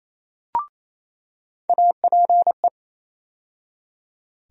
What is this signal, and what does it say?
Morse code, heard as a keyed tone, about 27 wpm, 700 Hz. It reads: APE
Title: USA Call Sign Postfixes - 3 Letters 28wpm